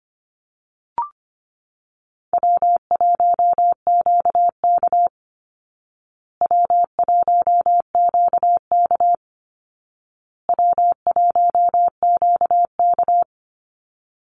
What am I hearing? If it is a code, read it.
Morse code, heard as a keyed tone, about 25 wpm, 700 Hz. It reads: W1QK W1QK W1QK
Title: Easy Call Signs Repeated 3x - 25wpm